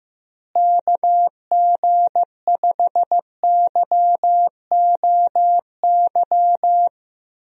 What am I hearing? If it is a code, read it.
Morse code, heard as a keyed tone, about 15 wpm, 700 Hz. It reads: KG5YOY
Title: USA Call Signs - 6 Characters - Repeated 3x - 15wpm